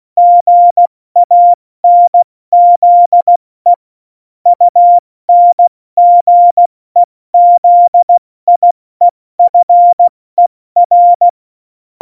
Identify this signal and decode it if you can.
Morse code, heard as a keyed tone, about 16 wpm, 700 Hz. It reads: GANZE UNGEZIEFER
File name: Morsekurs / Witze / Witz 22. Mai 2018, 70 F80